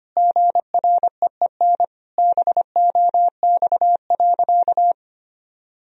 Morse code, heard as a keyed tone, about 25 wpm, 700 Hz. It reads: GREEN BOX.
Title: Sentences from Top 1000 Words - Review Part 2 - 25wpm